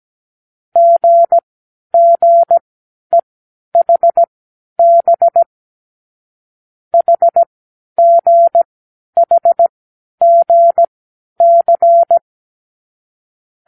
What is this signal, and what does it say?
Morse code, heard as a keyed tone, about 17 wpm, spaced as for 10 wpm, 680 Hz. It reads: GGEHB HGHGC
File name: k7qo / Lesson12(A-H)